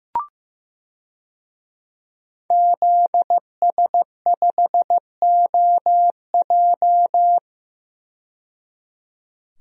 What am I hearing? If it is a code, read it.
Morse code, heard as a keyed tone, about 15 wpm, 700 Hz. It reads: ZS5OJ
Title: DXCC-100 Call Signs - 5 Characters - 15wpm